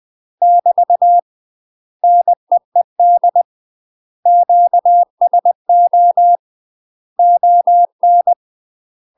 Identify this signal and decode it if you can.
Morse code, heard as a keyed tone, about 20 wpm, 700 Hz. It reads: <BT> NEED QSO ON
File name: UK MASTER QSOs TRAINING / 20WPM / UK MASTER QSOs 0038